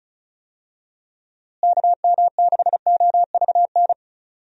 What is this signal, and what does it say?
Morse code, heard as a keyed tone, about 36 wpm, 700 Hz. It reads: KM6OVN
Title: USA Call Signs - 6 Characters - Repeated 2x - 35wpm